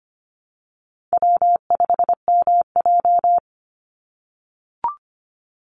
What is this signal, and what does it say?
Morse code, heard as a keyed tone, about 25 wpm, 700 Hz. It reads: W5MJ
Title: Easy Call Signs - 25wpm